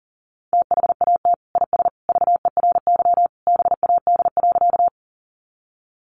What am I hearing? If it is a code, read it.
Morse code, heard as a keyed tone, about 40 wpm, 700 Hz. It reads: THAT IS VERY BAD.
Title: Sentences from Top 700 Words 40wpm